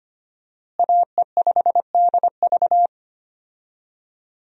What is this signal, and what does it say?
Morse code, heard as a keyed tone, about 25 wpm, 700 Hz. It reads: AE5DV
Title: USA Call Signs - 5 Characters - Repeated 2x - 25wpm